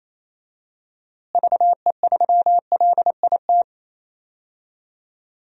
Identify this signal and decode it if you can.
Morse code, heard as a keyed tone, about 28 wpm, 700 Hz. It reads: VE3LIT
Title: DXCC-100 Call Signs - 6 Characters - Repeated 3x - 28wpm